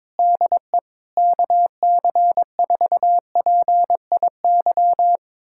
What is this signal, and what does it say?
Morse code, heard as a keyed tone, about 22 wpm, 700 Hz. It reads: DE KC4PIY
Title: USA Callsign DE Callsign - 6 Characters - 22wpm